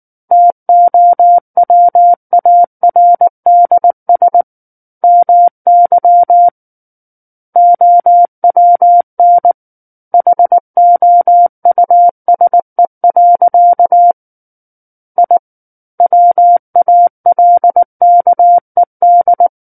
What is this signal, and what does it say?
Morse code, heard as a keyed tone, about 19 wpm, 700 Hz. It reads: TOWARDS MY OWN HOUSE. I WALKED
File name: Book One / 10-Chapter 10 - 19 WPM